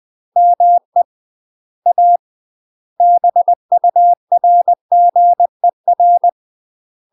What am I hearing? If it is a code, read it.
Morse code, heard as a keyed tone, about 20 wpm, 700 Hz. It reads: ME A BURGER
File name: UK MASTER QSOs TRAINING / 20WPM / UK MASTER QSOs 0013